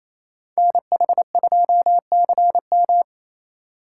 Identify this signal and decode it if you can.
Morse code, heard as a keyed tone, about 35 wpm, 700 Hz. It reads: NH2CM